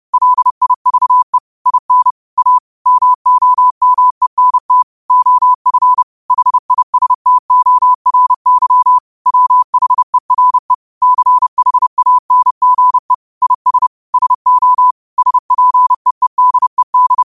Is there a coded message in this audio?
Yes: Morse code from 0.1 to 17.2 s, heard as a keyed tone, about 30 wpm, 1000 Hz. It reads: LIVE IN A MOMENT OF HISTORY WHERE CHANGE IS SO SPEEDED